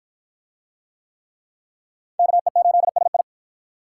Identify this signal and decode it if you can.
Morse code, heard as a keyed tone, about 46 wpm, 700 Hz. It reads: KE8SI